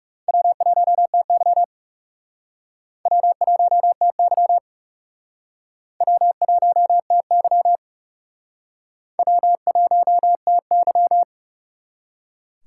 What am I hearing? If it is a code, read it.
Morse code, heard as a keyed tone, about 37 wpm, 700 Hz. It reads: W1TY W1TY W1TY W1TY
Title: Speed Racing - USA 1x2 Call Signs 30wpm